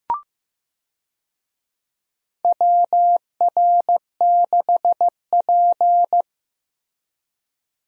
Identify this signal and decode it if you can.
Morse code, heard as a keyed tone, about 15 wpm, 700 Hz. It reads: WR6P
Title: USA Call Signs - 4 Characters - Repeated 2x - 15wpm